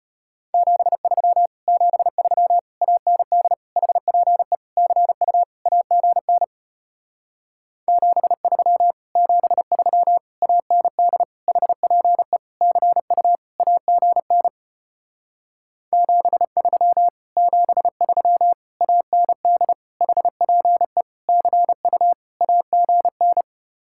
Morse code, heard as a keyed tone, about 33 wpm, 700 Hz. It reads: 73 73 AND HPE CU AGN 73 73 AND HPE CU AGN 73 73 AND HPE CU AGN